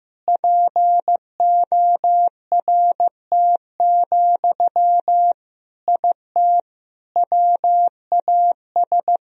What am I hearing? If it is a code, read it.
Morse code, heard as a keyed tone, about 15 wpm, 700 Hz. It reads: PORT, IT WAS
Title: Sentences from Top 1000 Words - Review Part 1 - 15wpm